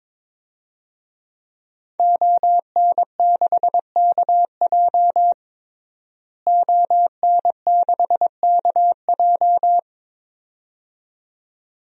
Morse code, heard as a keyed tone, about 22 wpm, 700 Hz. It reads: ON6KJ ON6KJ
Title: DXCC-100 Call Signs - 5 Characters - Repeated 2x - 22wpm